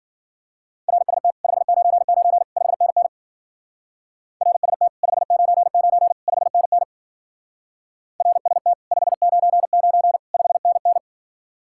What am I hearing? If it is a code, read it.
Morse code, heard as a keyed tone, about 50 wpm, 700 Hz. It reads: RST 599 5NN RST 599 5NN RST 599 5NN